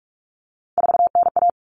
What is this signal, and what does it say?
Morse code, heard as a keyed tone, about 44 wpm, 700 Hz. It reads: 4NA
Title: Common Cut Numbers - 3-Digit 45wpm